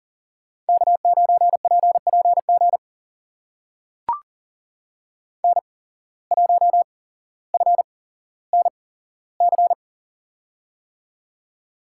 Morse code, heard as a keyed tone, about 42 wpm, spaced as for 17 wpm, 700 Hz. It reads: K9PPG N1FNC
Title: USA Call Signs - 5 Characters - 8x Character Spacing - 40wpm